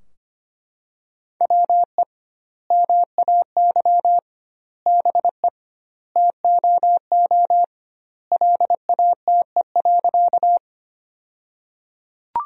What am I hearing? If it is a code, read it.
Morse code, heard as a keyed tone, about 25 wpm, 700 Hz. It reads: WE MAY BE TOO LATE.
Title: 2x Word-Spacing - Sentences from Top 300 Words - 25wpm